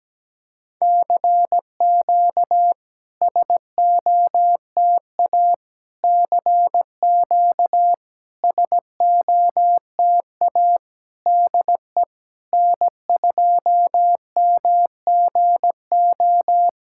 Morse code, heard as a keyed tone, about 17 wpm, 700 Hz. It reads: CQ SOTA CQ SOTA DE N2MGO
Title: SOTA QSO Practice 17wpm